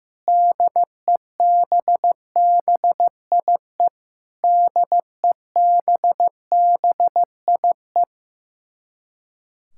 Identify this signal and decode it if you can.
Morse code, heard as a keyed tone, about 15 wpm, 700 Hz. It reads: DEBBIE DEBBIE
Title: Practice QSO - Exchange 1 - Beginning - 15wpm